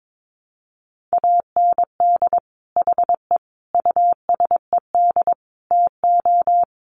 Morse code, heard as a keyed tone, about 22 wpm, 700 Hz. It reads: AND HE USED TO
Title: Sentences from Top 1200 Words 22wpm